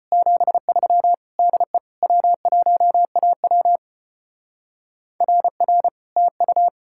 Morse code, heard as a keyed tone, about 31 wpm, 700 Hz. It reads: 73 DE W1AW RR TU
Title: Speed-Racing Common QSO Phrases Encore - 25wpm